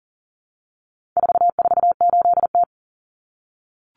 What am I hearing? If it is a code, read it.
Morse code, heard as a keyed tone, about 40 wpm, 700 Hz. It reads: 448T
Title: Common Cut Numbers - 4-Digit 40wpm